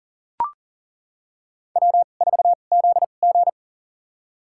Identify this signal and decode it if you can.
Morse code, heard as a keyed tone, about 40 wpm, spaced as for 29 wpm, 700 Hz. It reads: W4ZG
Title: USA Call Signs - 4 Characters - 2x Character Spacing - 40wpm